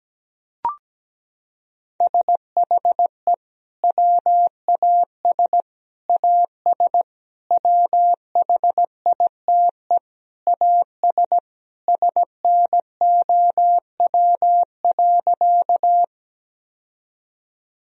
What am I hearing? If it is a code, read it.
Morse code, heard as a keyed tone, about 17 wpm, 700 Hz. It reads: SHE WAS AS WHITE AS SNOW.
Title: Sentences from Top 1000 Words - Review Part 3 - 17wpm